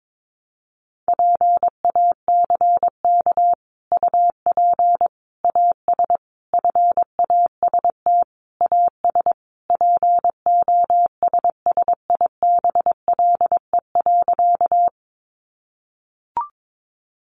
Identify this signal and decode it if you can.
Morse code, heard as a keyed tone, about 22 wpm, 700 Hz. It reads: PACK UP AS FAST AS POSSIBLE.
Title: Sentences from Top 1300 Words 22wpm